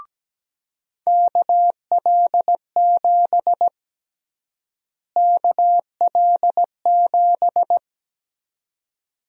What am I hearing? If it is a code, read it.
Morse code, heard as a keyed tone, about 17 wpm, 700 Hz. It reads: KL7 KL7
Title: USA Call Sign Prefixes - Repeated 2x - 17wpm